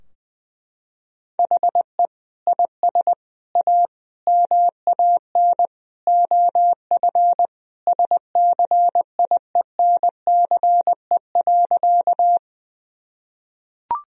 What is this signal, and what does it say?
Morse code, heard as a keyed tone, about 20 wpm, 700 Hz. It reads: HE IS A MAN OF SCIENCE.
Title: Sentences from Top 1000 Words - Review Part 2 - 20wpm